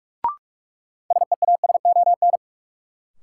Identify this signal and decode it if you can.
Morse code, heard as a keyed tone, about 45 wpm, 700 Hz. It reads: SEASON
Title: Sentences from Top 1400 Words - New Word Drill - 45wpm